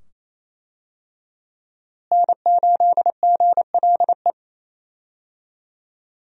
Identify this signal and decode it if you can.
Morse code, heard as a keyed tone, about 28 wpm, 700 Hz. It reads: N8GLE
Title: USA Call Signs - 5 Characters - Repeated 2x - 28wpm